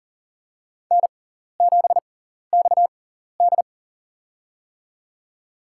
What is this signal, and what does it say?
Morse code, heard as a keyed tone, about 37 wpm, spaced as for 14 wpm, 700 Hz. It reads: N7XD
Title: USA Call Signs - 4 Characters - 6x Character Spacing - 40wpm